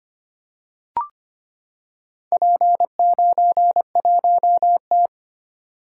Morse code, heard as a keyed tone, about 25 wpm, 690 Hz. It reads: P91T
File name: practice / 25wpm / Four Letter-Number Combinations - 25wpm